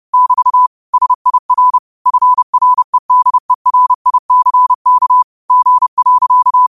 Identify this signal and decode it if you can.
Morse code, heard as a keyed tone, about 30 wpm, 1000 Hz. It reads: <BT> SIR FREDERICK G.